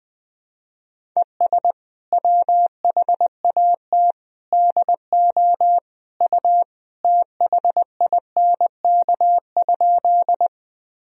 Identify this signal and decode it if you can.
Morse code, heard as a keyed tone, about 20 wpm, 700 Hz. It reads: ES WHAT DO U THINK?
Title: Sentences from Top 200 Words and Abbreviations 20wpm